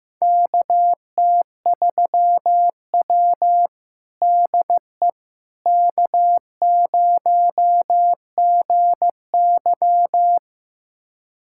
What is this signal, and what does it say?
Morse code, heard as a keyed tone, about 16 wpm, 700 Hz. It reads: KT3W DE K0GY
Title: USA Callsign DE Callsign - 4 Characters - 15wpm